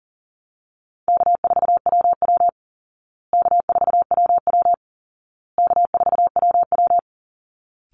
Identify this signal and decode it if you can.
Morse code, heard as a keyed tone, about 37 wpm, 700 Hz. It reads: K4WW K4WW K4WW
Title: Easy Call Signs Repeated 3x - 40wpm